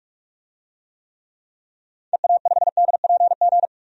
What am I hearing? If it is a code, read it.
Morse code, heard as a keyed tone, about 46 wpm, 700 Hz. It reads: EA5DPG
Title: DXCC-100 Call Signs - 6 Characters - 45wpm